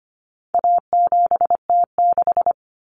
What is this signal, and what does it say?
Morse code, heard as a keyed tone, about 25 wpm, 700 Hz. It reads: A7T6
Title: Common Cut Numbers - 4-Digit 25wpm